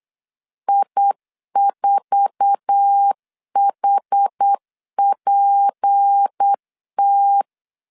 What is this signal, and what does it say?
Morse code, heard as a keyed tone, about 8 wpm, 800 Hz. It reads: I4HPT
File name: LEZIONI / 8 YJ45 / 8 YJ4 8wpm 3246